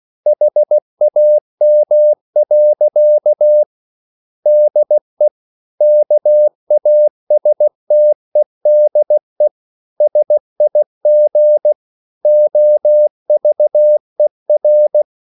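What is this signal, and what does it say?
Morse code, heard as a keyed tone, about 16 wpm, 600 Hz. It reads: HAM. DE KASTEDE SIG OVER
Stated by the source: Slow (QRS) Morse code at 16 wpm, from CW med Gnister.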